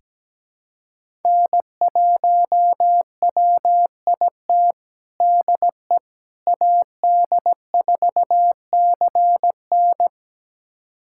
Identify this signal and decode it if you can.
Morse code, heard as a keyed tone, about 17 wpm, 700 Hz. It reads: N1WIT DE AD4CN
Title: USA Callsign DE Callsign - 5 Characters - 17wpm